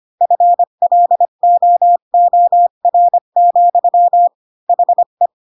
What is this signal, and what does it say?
Morse code, heard as a keyed tone, about 25 wpm, 700 Hz. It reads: FLOOR, HE
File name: Pinocchio / Pinocchio ch7 20 25 ik1hge